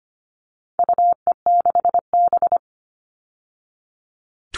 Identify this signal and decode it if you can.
Morse code, heard as a keyed tone, about 25 wpm, 700 Hz. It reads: UE6B
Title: All Cut Numbers - 4-Digit 25wpm